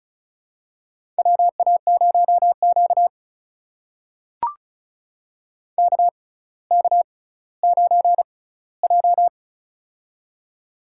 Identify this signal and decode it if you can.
Morse code, heard as a keyed tone, about 34 wpm, spaced as for 18 wpm, 700 Hz. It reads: WA0Q KK9J
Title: USA Call Signs - 4 Characters - 6x Character Spacing - 35wpm